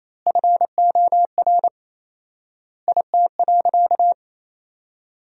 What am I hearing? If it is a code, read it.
Morse code, heard as a keyed tone, about 28 wpm, 700 Hz. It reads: FOR IT.
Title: 4x Word-Spacing - Sentences from Top 200 Words - 28wpm